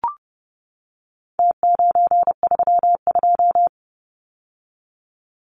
Morse code, heard as a keyed tone, about 30 wpm, 700 Hz. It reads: T932
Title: Numbers - Contest Style - 4-Digit 30wpm